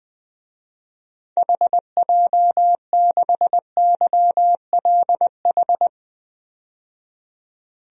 Morse code, heard as a keyed tone, about 21 wpm, 700 Hz. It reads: HJ6YLH
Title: DXCC-100 Call Signs - 6 Characters - 20wpm